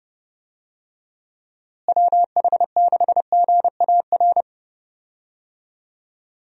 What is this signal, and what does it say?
Morse code, heard as a keyed tone, about 30 wpm, 700 Hz. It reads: WH6GAR
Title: USA Call Signs - 6 Characters - Repeated 3x - 30wpm